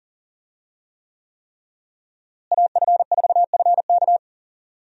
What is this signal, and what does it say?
Morse code, heard as a keyed tone, about 37 wpm, 700 Hz. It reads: AF4FK
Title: USA Call Signs - 5 Characters - Repeated 3x - 40wpm